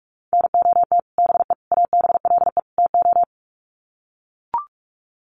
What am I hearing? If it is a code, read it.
Morse code, heard as a keyed tone, about 44 wpm, 700 Hz. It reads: NOT BE ABLE TO
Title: Sets of 4 Words 45wpm